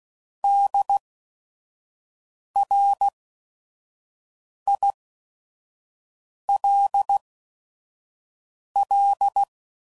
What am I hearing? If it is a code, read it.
Morse code, heard as a keyed tone, about 16 wpm, spaced as for 5 wpm, 800 Hz. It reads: DRILL
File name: Prefixes / By First Letter / Ds